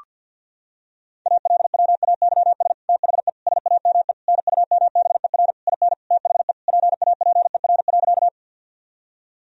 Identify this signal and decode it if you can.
Morse code, heard as a keyed tone, about 51 wpm, 700 Hz. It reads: ALWAYS THE SAME NUMBER IN THE PAPER.
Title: Sentences from Top 1000 Words - Review Part 1 - 50wpm